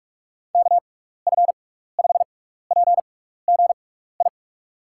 Morse code, heard as a keyed tone, about 46 wpm, spaced as for 15 wpm, 700 Hz. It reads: KF5PGI
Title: USA Call Signs - 6 Characters - 6x Character Spacing - 45wpm